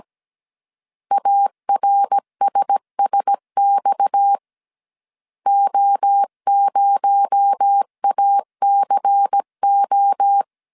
Morse code, heard as a keyed tone, about 17 wpm, 800 Hz. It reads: ARSSX O0ACO
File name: 16wpm